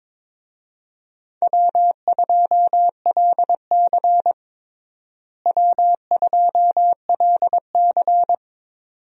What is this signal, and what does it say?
Morse code, heard as a keyed tone, about 22 wpm, 700 Hz. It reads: W2LC W2LC
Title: USA Call Signs - 4 Characters - Repeated 2x - 22wpm